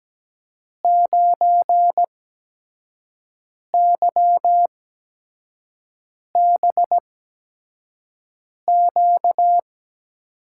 Morse code, heard as a keyed tone, about 17 wpm, spaced as for 5 wpm, 700 Hz. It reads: 9YBQ
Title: USA Call Signs - 5 Characters - 8x Character Spacing - 17wpm